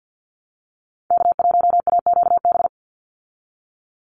Morse code, heard as a keyed tone, about 48 wpm, 700 Hz. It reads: K1AQB
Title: Call Signs Repeated 3x - 50wpm